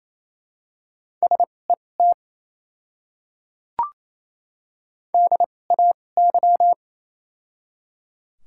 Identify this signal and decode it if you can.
Morse code, heard as a keyed tone, about 28 wpm, spaced as for 20 wpm, 690 Hz. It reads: SET DAY
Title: 2x Character Spacing - Top 500 Words - 28wpm